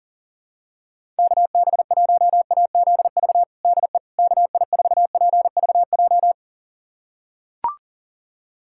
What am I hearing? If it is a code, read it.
Morse code, heard as a keyed tone, about 39 wpm, 700 Hz. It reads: KB1AZV DE KI4PVJ
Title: USA Callsign DE Callsign - 6 Characters - 40wpm